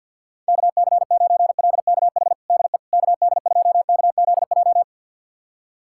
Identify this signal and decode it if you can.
Morse code, heard as a keyed tone, about 45 wpm, 700 Hz. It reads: KC9LKH DE KD2KZJ
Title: USA Callsign DE Callsign 50wpm